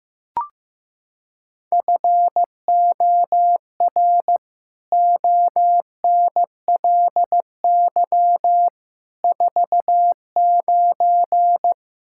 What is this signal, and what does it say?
Morse code, heard as a keyed tone, about 16 wpm, 700 Hz. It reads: FOR ONLY 49
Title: Pangram Challenge v2 - 15wpm